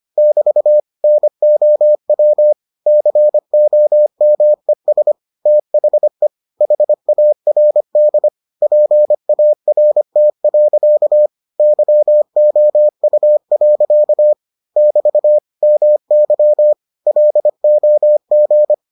Morse code, heard as a keyed tone, about 25 wpm, 600 Hz. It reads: <BT> NOW COMES THE HARD PART. YOU. <BT> MY LOG